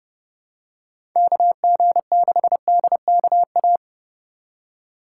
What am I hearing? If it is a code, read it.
Morse code, heard as a keyed tone, about 32 wpm, 700 Hz. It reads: KG6DKA
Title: USA Call Signs - 6 Characters - 3x Character Spacing - 30wpm